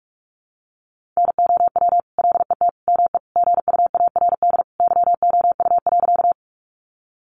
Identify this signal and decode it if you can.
Morse code, heard as a keyed tone, about 44 wpm, 700 Hz. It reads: NOW LET ME GUARD YOU.
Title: Sentences from Top 700 Words 45wpm